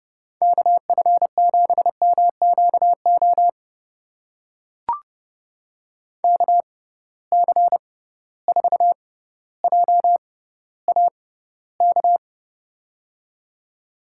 Morse code, heard as a keyed tone, about 32 wpm, spaced as for 16 wpm, 700 Hz. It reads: KF7MQO KC4JAK
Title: USA Call Signs - 6 Characters - 6x Character Spacing - 30wpm